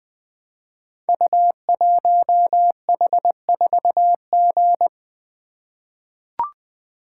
Morse code, heard as a keyed tone, about 20 wpm, 700 Hz. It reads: U1H4G
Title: Code Groups - 5 Letter:Numbers - 20wpm